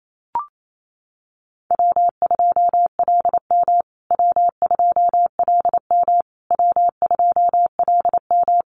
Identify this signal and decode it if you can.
Morse code, heard as a keyed tone, about 28 wpm, 700 Hz. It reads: W2LM W2LM W2LM
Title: USA Call Signs Repeated 3x 28wpm